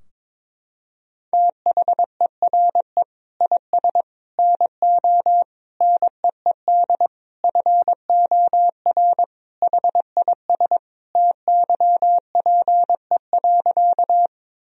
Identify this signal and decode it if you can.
Morse code, heard as a keyed tone, about 22 wpm, 700 Hz. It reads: THERE IS NO NEED FOR HIS TYPE.
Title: Sentences from Top 1000 Words - Review Part 3 - 22wpm